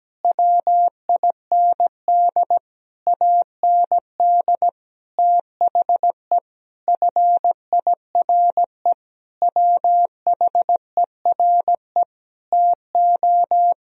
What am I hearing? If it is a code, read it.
Morse code, heard as a keyed tone, about 17 wpm, 700 Hz. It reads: WIND AND THE FIRE WHERE TO
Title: Sentences from Top 1000 Words - Review Part 2 - 17wpm